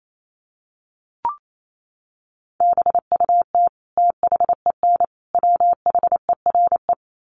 Morse code, heard as a keyed tone, about 28 wpm, 700 Hz. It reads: BUT THEN WHERE
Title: Sentences from Top 1300 Words 28wpm